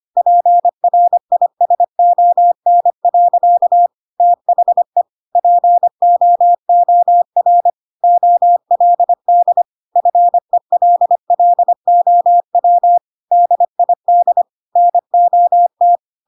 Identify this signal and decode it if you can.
Morse code, heard as a keyed tone, about 25 wpm, 700 Hz. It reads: PRISON. THE POOR OLD FELLOW DID NOT
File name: Pinocchio / Pinocchio ch3 25 25 ik1hge